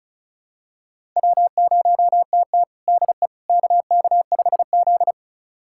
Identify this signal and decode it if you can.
Morse code, heard as a keyed tone, about 34 wpm, 700 Hz. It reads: W0TT DE KK5Z
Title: USA Callsign DE Callsign - 4 Characters - 35wpm